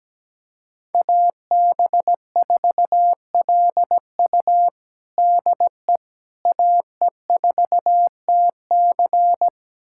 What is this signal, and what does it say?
Morse code, heard as a keyed tone, about 17 wpm, 700 Hz. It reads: AB4LU DE AE4TC
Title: USA Callsign DE Callsign - 5 Characters - 17wpm